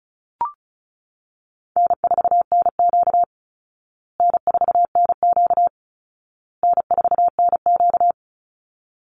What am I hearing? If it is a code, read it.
Morse code, heard as a keyed tone, about 34 wpm, 700 Hz. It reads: N4NQ N4NQ N4NQ
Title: Easy Call Signs Repeated 3x - 35wpm